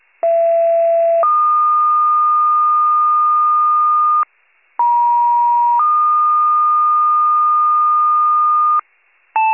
Two-tone page, heard as a one-second tone, then a three-second tone, about 670 then 1220 Hz, about 950 then 1220 Hz. It is heard from 0.2 to 8.8 s.